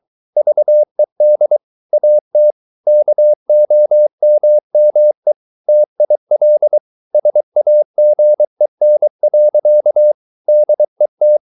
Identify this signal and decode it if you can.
Morse code, heard as a keyed tone, about 23 wpm, 600 Hz. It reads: VED AT KOMME TIL SAGEN. DET
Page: Ferie 23 wpm